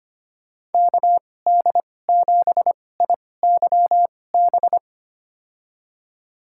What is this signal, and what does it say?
Morse code, heard as a keyed tone, about 25 wpm, spaced as for 18 wpm, 700 Hz. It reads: KD7IYB
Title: USA Call Signs - 6 Characters - 2x Character Spacing - 25wpm